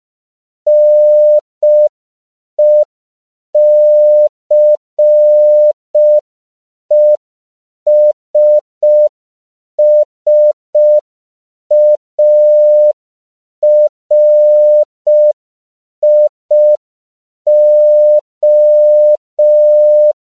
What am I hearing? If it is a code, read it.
Morse code, heard as a keyed tone, about 5 wpm, 600 Hz. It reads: NECESSARIO
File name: CW / 5PPM